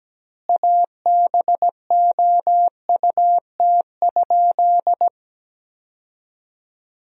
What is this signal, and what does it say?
Morse code, heard as a keyed tone, about 17 wpm, 700 Hz. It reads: ABOUT?
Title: 2x Word-Spacing - Sentences from Top 100 Words - 17wpm